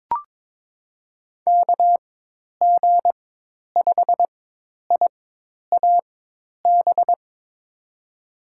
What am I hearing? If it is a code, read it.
Morse code, heard as a keyed tone, about 22 wpm, spaced as for 10 wpm, 700 Hz. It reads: KG5IAB
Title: USA Call Signs - 6 Characters - 4x Character Spacing - 22wpm